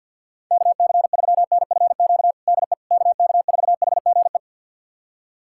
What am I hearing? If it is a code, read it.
Morse code, heard as a keyed tone, about 51 wpm, 700 Hz. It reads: KC3NFQ DE KK4HGE
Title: USA Callsign DE Callsign - 6 Characters - 50wpm